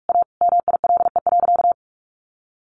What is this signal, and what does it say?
Morse code, heard as a keyed tone, about 44 wpm, 700 Hz. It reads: A MILE.
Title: Sentences from Top 1300 Words 45wpm